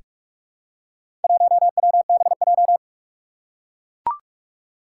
Morse code, heard as a keyed tone, about 44 wpm, 700 Hz. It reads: 1WBJ
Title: Four Challenging Letter-Number Combinations 45wpm